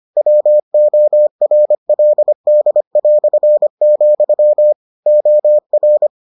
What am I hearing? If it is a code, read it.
Morse code, heard as a keyed tone, about 25 wpm, 600 Hz. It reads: WORLD", OR